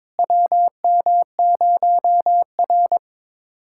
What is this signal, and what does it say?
Morse code, heard as a keyed tone, about 22 wpm, 700 Hz. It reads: WM0R
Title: USA Callsign DE Callsign - 4 Characters - 22wpm